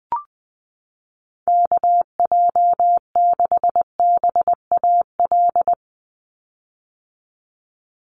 Morse code, heard as a keyed tone, about 20 wpm, 700 Hz. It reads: KJ6BAL
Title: USA Call Signs 20wpm